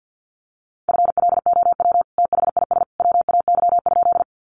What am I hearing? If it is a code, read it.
Morse code, heard as a keyed tone, about 50 wpm, 700 Hz. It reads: FLOW THIS WAY?
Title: Sentences from Top 1200 Words 50wpm